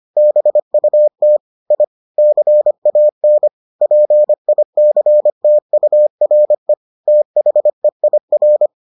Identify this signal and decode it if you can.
Morse code, heard as a keyed tone, about 25 wpm, 600 Hz. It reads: BUT I CAN PICTURE THEIR